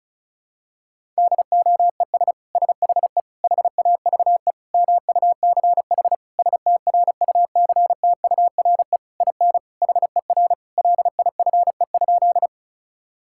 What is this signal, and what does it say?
Morse code, heard as a keyed tone, about 34 wpm, 700 Hz. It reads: DOES SHE HAVE MUCH STRUCTURE IN HER LIFE?
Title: Sentences from Top 1000 Words - Review Part 6 - 35wpm